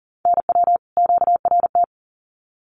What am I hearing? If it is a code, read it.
Morse code, heard as a keyed tone, about 38 wpm, 700 Hz. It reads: NW QRT
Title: Common QSO Phrases 40wpm